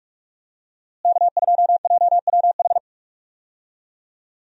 Morse code, heard as a keyed tone, about 41 wpm, 700 Hz. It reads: K2JWH
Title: USA Call Signs - 5 Characters - Repeated 2x - 45wpm